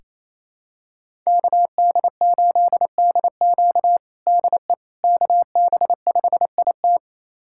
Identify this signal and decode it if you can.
Morse code, heard as a keyed tone, about 27 wpm, 700 Hz. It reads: KD8DQ DE KB5IT
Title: USA Callsign DE Callsign - 5 Characters - 28wpm